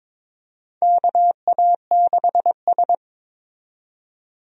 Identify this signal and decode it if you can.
Morse code, heard as a keyed tone, about 22 wpm, 700 Hz. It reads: KA6S
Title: USA Call Signs - 4 Characters - 22wpm